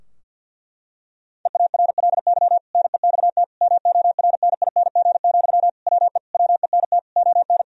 Morse code, heard as a keyed tone, about 50 wpm, 700 Hz. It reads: EARLY NEXT MORNING, WE WENT ON
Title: Sentences from Top 1000 Words - Review Part 4 - 50wpm